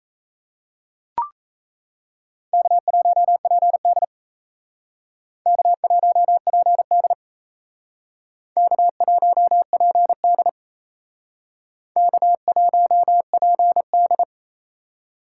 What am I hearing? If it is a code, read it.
Morse code, heard as a keyed tone, about 36 wpm, 700 Hz. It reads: K1PD K1PD K1PD K1PD